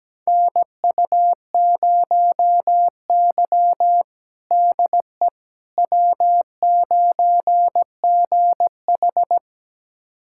Morse code, heard as a keyed tone, about 17 wpm, 700 Hz. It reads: NU0Y DE W9GH
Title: USA Callsign DE Callsign - 4 Characters - 17wpm